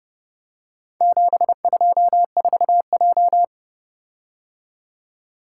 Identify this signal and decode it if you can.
Morse code, heard as a keyed tone, about 29 wpm, 700 Hz. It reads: 724J